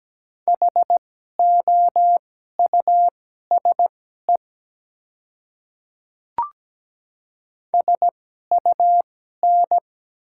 Morse code, heard as a keyed tone, about 17 wpm, spaced as for 12 wpm, 690 Hz. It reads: HOUSE SUN
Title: 2x Character Spacing - Top 300 Words - 17wpm